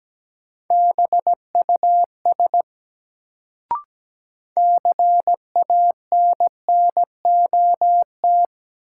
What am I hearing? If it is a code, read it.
Morse code, heard as a keyed tone, about 17 wpm, 700 Hz. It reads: BUS CANNOT
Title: Sentences from Top 1000 Words - Review Part 6 - New Word Drill - 17wpm